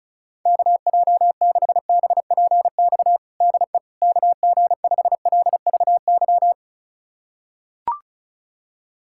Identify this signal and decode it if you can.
Morse code, heard as a keyed tone, about 36 wpm, 700 Hz. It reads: KJ6BPX DE KG5LVY
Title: USA Callsign DE Callsign - 6 Characters - 35wpm